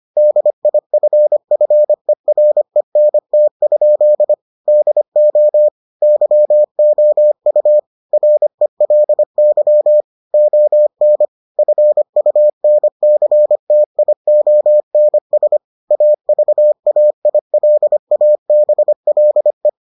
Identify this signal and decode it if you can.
Morse code, heard as a keyed tone, about 25 wpm, 600 Hz. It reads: DIFFERENT? DO YOU RELY ON FUNCTIONS AVAILABLE